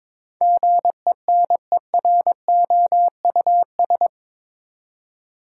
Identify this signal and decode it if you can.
Morse code, heard as a keyed tone, about 22 wpm, 700 Hz. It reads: GENEROUS
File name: practice / 22wpm / Eight Letter Words - 22wpm